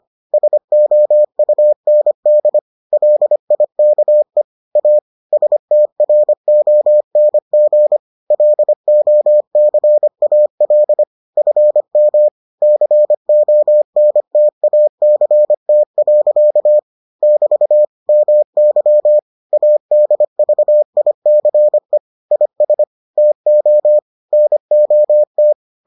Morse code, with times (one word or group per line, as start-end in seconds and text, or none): 0.3-2.6 SOUND
2.9-4.4 LIKE
4.8-5.0 A
5.3-8.0 STRONG
8.3-11.0 LOCAL
11.4-12.3 FM
12.6-16.8 CONTACT.
17.2-17.9 <BT>
18.1-19.2 MY
19.5-22.0 ADVICE
22.3-22.8 IS
23.2-24.0 TO
24.3-25.5 NOT